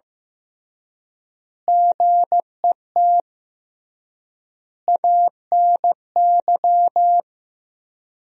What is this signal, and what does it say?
Morse code, heard as a keyed tone, about 15 wpm, 700 Hz. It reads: GET ANY
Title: 3x Word-Spacing - Sentences from Top 400 Words - 15wpm